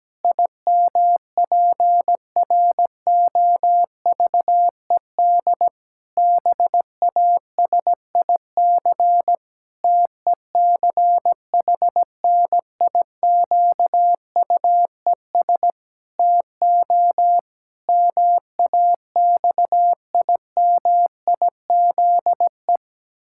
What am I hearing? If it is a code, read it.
Morse code, heard as a keyed tone, about 17 wpm, 700 Hz. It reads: IMPROVED BASIC TECHNIQUES TO MAXIMIZE
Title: Pangram Challenge v2 - 17wpm